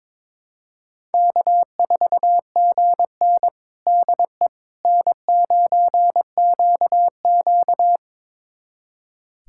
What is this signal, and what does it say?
Morse code, heard as a keyed tone, about 21 wpm, 700 Hz. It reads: K4GN DE N9QQ
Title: USA Callsign DE Callsign - 4 Characters - 22wpm